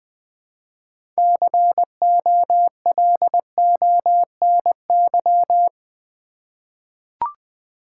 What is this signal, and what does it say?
Morse code, heard as a keyed tone, about 20 wpm, 700 Hz. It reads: COLONY
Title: 6 Letter Words - First 3 Letters Given - 20wpm